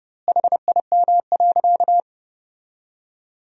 Morse code, heard as a keyed tone, about 30 wpm, 700 Hz. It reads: HIM.
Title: 2x Word-Spacing - Sentences from Top 300 Words - 30wpm